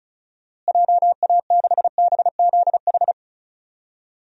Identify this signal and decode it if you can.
Morse code, heard as a keyed tone, about 34 wpm, 700 Hz. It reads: JA6BZH
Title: DXCC-100 Call Signs - 6 Characters - Repeated 2x - 35wpm